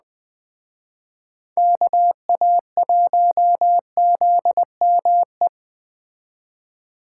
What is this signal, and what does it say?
Morse code, heard as a keyed tone, about 21 wpm, 700 Hz. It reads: KA1ZME